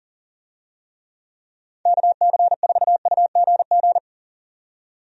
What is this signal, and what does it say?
Morse code, heard as a keyed tone, about 40 wpm, 700 Hz. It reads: KC4UGG
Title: USA Call Signs - 6 Characters - 40wpm